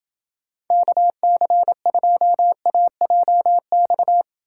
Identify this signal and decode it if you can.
Morse code, heard as a keyed tone, about 28 wpm, 700 Hz. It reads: KC2AJX